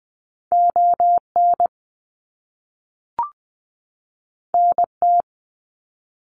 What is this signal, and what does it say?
Morse code, heard as a keyed tone, about 20 wpm, 700 Hz. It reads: ON NT
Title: Canadian Provinces and Territories 20wpm